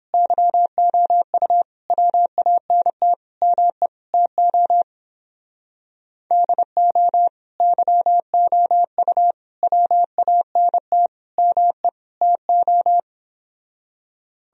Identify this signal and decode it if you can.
Morse code, heard as a keyed tone, about 29 wpm, 700 Hz. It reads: YOU WANT ME TO DO YOU WANT ME TO